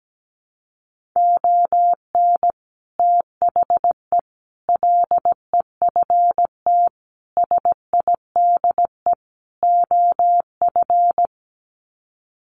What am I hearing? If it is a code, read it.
Morse code, heard as a keyed tone, about 17 wpm, 700 Hz. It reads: ON THE LEFT SIDE OF
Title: Sets of 5 Words Encore2 17wpm